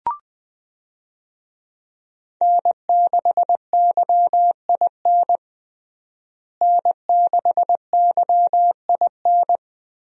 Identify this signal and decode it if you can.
Morse code, heard as a keyed tone, about 20 wpm, 700 Hz. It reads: N6YIN N6YIN
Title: USA Call Signs - 5 Characters - Repeated 2x - 20wpm